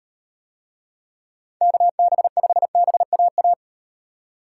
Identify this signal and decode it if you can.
Morse code, heard as a keyed tone, about 36 wpm, 700 Hz. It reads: KB5BAA